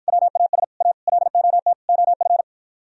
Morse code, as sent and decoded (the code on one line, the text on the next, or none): .-- .- ... / .- / .-.. --- - / --- ..-.
WAS A LOT OF